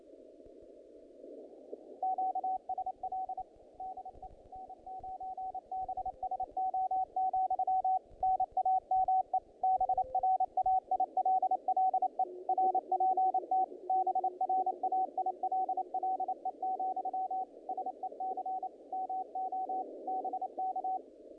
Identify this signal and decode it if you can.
Morse code, heard as a keyed tone, about 29 wpm, 750 Hz, with radio noise. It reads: QSL DE N9BSO, NAME BRAILLE RPT BRAILLE, SEC MO BK
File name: contests / naqp-cw-k5tr-op-k5pi- -2000